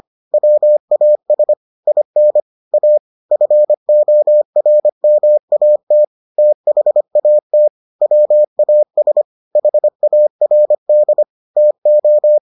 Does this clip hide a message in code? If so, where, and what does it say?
Yes: Morse code from 0.3 to 12.4 s, heard as a keyed tone, about 25 wpm, 600 Hz. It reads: WAS IN A FORMAT THAT WAS HARD TO